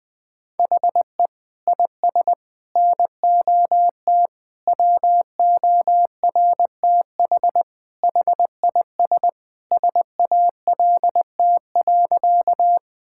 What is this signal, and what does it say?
Morse code, heard as a keyed tone, about 20 wpm, 700 Hz. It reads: HE IS NOT WORTH HIS SALT.
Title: Sentences from Top 1000 Words - Review Part 5 - 20wpm